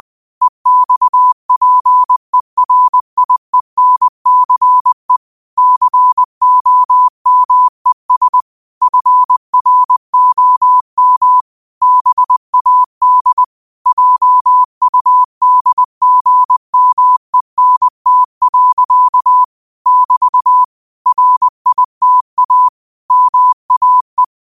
Morse code, heard as a keyed tone, about 20 wpm, 1000 Hz. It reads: EXPERIENCE COMES FROM BAD JUDGMENT. <BT> RITA MAE
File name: qotdfiles / 20wpm